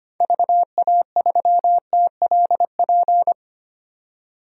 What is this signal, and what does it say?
Morse code, heard as a keyed tone, about 25 wpm, 700 Hz. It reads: VA3TLP
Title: DXCC-100 Call Signs - 6 Characters - 25wpm